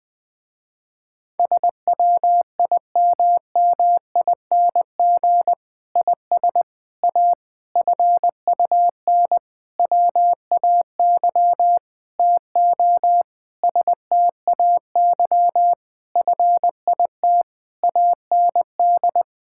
Morse code, heard as a keyed tone, about 20 wpm, 700 Hz. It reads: SWIMMING IS A FUN WAY TO STAY FIT AND
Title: Difficult Sentences with Hints - 20wpm